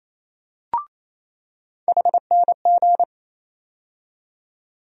Morse code, heard as a keyed tone, about 28 wpm, 700 Hz. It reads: HNG
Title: USA Call Sign Postfixes - 3 Letters 28wpm